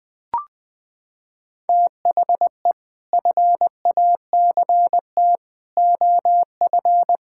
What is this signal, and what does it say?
Morse code, heard as a keyed tone, about 21 wpm, 700 Hz. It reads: THE FACT OF
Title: Sets of 6 Words - 20wpm